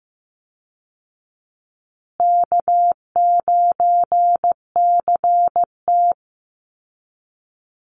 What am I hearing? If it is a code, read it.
Morse code, heard as a keyed tone, about 15 wpm, 700 Hz. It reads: K9CT
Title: Easy Call Signs Repeated 3x - 15wpm